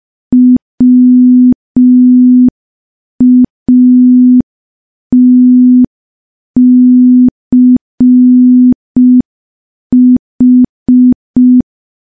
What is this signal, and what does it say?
Morse code, heard as a keyed tone, about 5 wpm, 250 Hz. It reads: WATCH
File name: morse / morse-training-05WPM